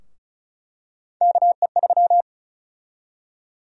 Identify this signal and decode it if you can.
Morse code, heard as a keyed tone, about 34 wpm, 700 Hz. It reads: KE3
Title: USA Call Sign Prefixes - 35wpm